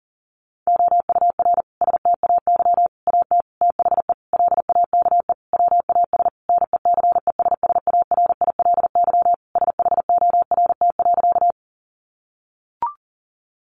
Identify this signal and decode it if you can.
Morse code, heard as a keyed tone, about 40 wpm, 700 Hz. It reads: OUR STAY AT THE LAKE WAS NECESSARILY SHORT.
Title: Sentences from Top 1400 Words 40wpm